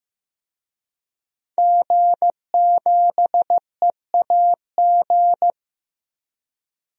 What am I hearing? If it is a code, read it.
Morse code, heard as a keyed tone, about 15 wpm, 700 Hz. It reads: G7EAG
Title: DXCC-100 Call Signs - 5 Characters - 15wpm